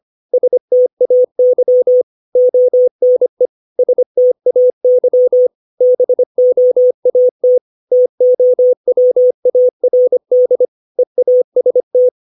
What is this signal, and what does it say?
Morse code, heard as a keyed tone, about 25 wpm, 500 Hz. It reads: STAY ONE STAY BOAT TOWARD EAST
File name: 632 words / 632 Words 25WPM